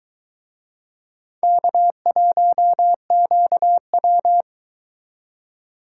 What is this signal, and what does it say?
Morse code, heard as a keyed tone, about 21 wpm, 700 Hz. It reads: K1QW